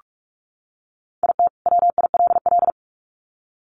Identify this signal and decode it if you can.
Morse code, heard as a keyed tone, about 44 wpm, 700 Hz. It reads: IT WILL
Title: Sets of 2 Words 45wpm